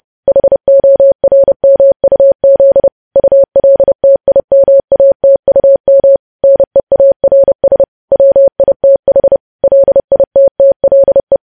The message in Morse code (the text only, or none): HORMUZ ULTIMATUM NEARS WITH LITTLE